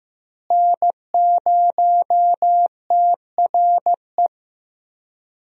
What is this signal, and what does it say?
Morse code, heard as a keyed tone, about 15 wpm, 700 Hz. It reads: N0TRE
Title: USA Call Signs - 5 Characters - Repeated 2x - 15wpm